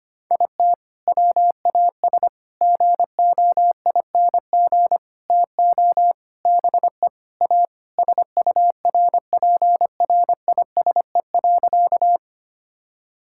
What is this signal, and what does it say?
Morse code, heard as a keyed tone, about 25 wpm, 700 Hz. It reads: IT WAS GOING TO BE A SURPRISE.
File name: practice / 25wpm / Sentences with 4-Word Phrases - 25wpm